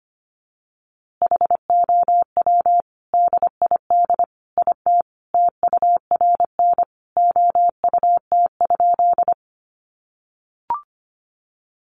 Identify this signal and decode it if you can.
Morse code, heard as a keyed tone, about 25 wpm, 700 Hz. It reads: HOW DID IT TURN OUT?
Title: A Sound Of Thunder 25wpm